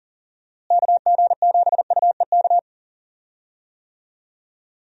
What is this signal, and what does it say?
Morse code, heard as a keyed tone, about 37 wpm, 700 Hz. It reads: KG7UEK
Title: USA Call Signs - 6 Characters - Repeated 3x - 40wpm